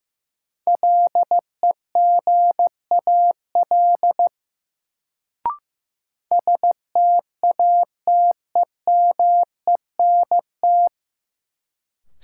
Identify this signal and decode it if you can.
Morse code, heard as a keyed tone, about 15 wpm, 700 Hz. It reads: LEGAL STATEMENT
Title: Sentences from Top 1200 Words - New Word Drill Alt - 15wpm